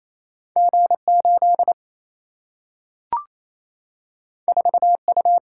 Morse code, heard as a keyed tone, about 28 wpm, 700 Hz. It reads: G 8 4 U
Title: Single-Letter Single-Number 28wpm